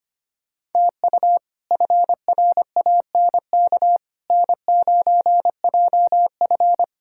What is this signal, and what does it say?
Morse code, heard as a keyed tone, about 25 wpm, 700 Hz. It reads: TU FRANK N9JF
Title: K1USN SST QSO Practice - 25wpm